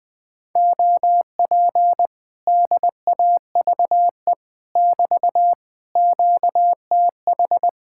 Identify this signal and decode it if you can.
Morse code, heard as a keyed tone, about 20 wpm, 700 Hz. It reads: OP DAVE <BT> QTH
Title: Speed-Racing Common QSO Phrases - 17wpm